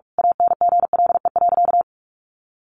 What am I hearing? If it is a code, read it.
Morse code, heard as a keyed tone, about 44 wpm, 700 Hz. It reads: ANGLE.
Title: Sentences from Top 1500 Words 45wpm